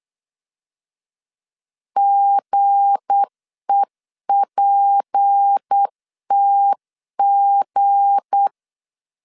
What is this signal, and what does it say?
Morse code, heard as a keyed tone, about 8 wpm, 800 Hz. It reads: GEPTG
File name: LEZIONI / 7 GF9Z / 7 G 8wpm 3118